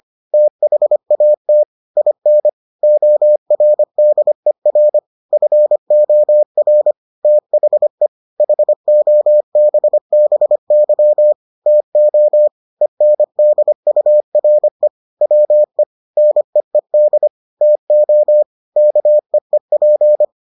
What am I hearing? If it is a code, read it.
Morse code, heard as a keyed tone, about 25 wpm, 600 Hz. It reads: THAT IN ORDER FOR THE HOBBY TO ENDURE WE NEED TO KEEP